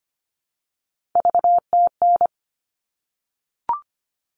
Morse code, heard as a keyed tone, about 25 wpm, 700 Hz. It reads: VTN
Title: All Cut Numbers - 3-Digit 25wpm